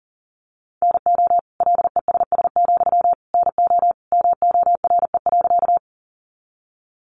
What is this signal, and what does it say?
Morse code, heard as a keyed tone, about 37 wpm, 700 Hz. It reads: NO LESS, NO MORE.
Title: Sentences from Top 400 Words 40wpm